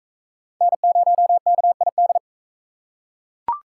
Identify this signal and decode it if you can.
Morse code, heard as a keyed tone, about 37 wpm, 700 Hz. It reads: N0KID